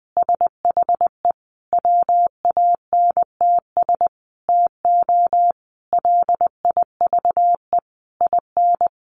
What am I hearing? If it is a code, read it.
Morse code, heard as a keyed tone, about 20 wpm, 700 Hz. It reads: SHE WANTS TO LIVE IN
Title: Easy Phrases 20wpm